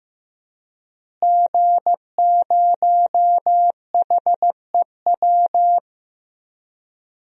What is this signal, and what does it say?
Morse code, heard as a keyed tone, about 15 wpm, 700 Hz. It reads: G0HEW
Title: DXCC-100 Call Signs - 5 Characters - Repeated 3x - 15wpm